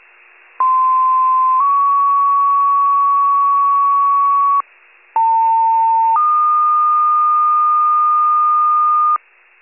Two-tone page, heard as a one-second tone, then a three-second tone, about 1040 then 1120 Hz, about 900 then 1220 Hz. It is heard from 0.6 to 9.2 s.
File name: pager / 78 RESCUE / wcalert